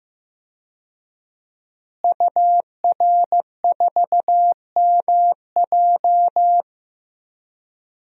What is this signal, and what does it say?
Morse code, heard as a keyed tone, about 15 wpm, 700 Hz. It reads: UR4MJ
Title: DXCC-100 Call Signs - 5 Characters - Repeated 2x - 15wpm